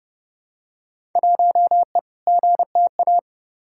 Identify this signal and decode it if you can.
Morse code, heard as a keyed tone, about 30 wpm, 700 Hz. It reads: 1E GTA
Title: Field Day Exchanges - 30wpm